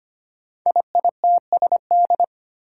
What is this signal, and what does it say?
Morse code, heard as a keyed tone, about 25 wpm, 700 Hz. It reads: IITSD
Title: Code Groups - 5 Letters - 25wpm